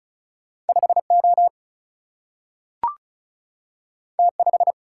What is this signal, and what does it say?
Morse code, heard as a keyed tone, about 35 wpm, 700 Hz. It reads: 5O T5
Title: Single-Letter Single-Number 35wpm